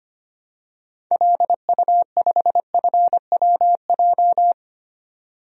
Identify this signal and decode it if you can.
Morse code, heard as a keyed tone, about 25 wpm, 700 Hz. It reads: LU5FWJ
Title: DXCC-100 Call Signs - 6 Characters - 25wpm